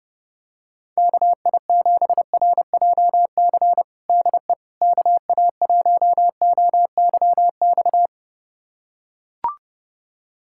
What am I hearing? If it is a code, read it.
Morse code, heard as a keyed tone, about 28 wpm, 700 Hz. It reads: KI7RJC DE KA1OYX
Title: USA Callsign DE Callsign - 6 Characters - 30wpm